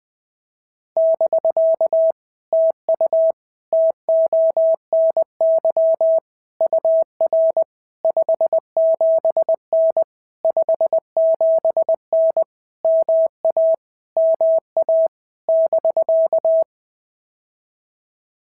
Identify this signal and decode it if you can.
Morse code, heard as a keyed tone, about 20 wpm, 650 Hz. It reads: <BK> TU TONY UR 57N 57N MA MA <BK>
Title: POTA QSO Practice - 20wpm